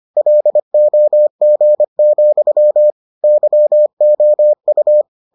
Morse code, heard as a keyed tone, about 25 wpm, 600 Hz. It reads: LOG, YOU